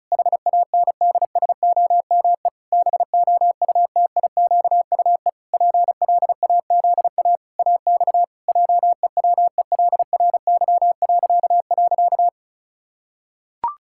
Morse code, heard as a keyed tone, about 36 wpm, 700 Hz. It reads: HANDSOME BOUTIQUE PLAZA AX JEWELRY..
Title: Pangram Challenge v2 - 35wpm